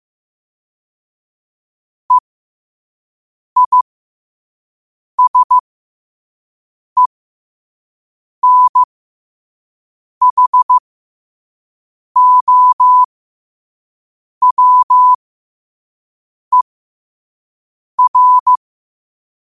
Morse code, heard as a keyed tone, about 15 wpm, spaced as for 5 wpm, 1000 Hz. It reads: EISENHOWER
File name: qotdfiles / 5wpm